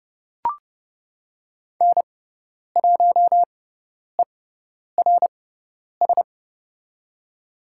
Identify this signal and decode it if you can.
Morse code, heard as a keyed tone, about 30 wpm, spaced as for 10 wpm, 700 Hz. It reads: N1ERS
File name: Lesson 16g - Letter c - Call Signs 30wpm